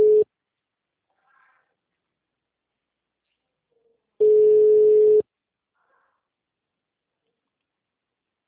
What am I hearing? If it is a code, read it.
Ringback tone (Europe, 425 Hz).